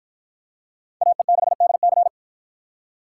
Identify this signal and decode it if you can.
Morse code, heard as a keyed tone, about 50 wpm, 700 Hz. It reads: AE6DC